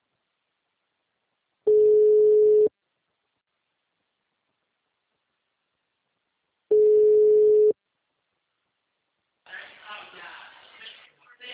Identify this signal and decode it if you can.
Ringback tone (Europe, 425 Hz), with speech.
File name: out- -2017- - - .211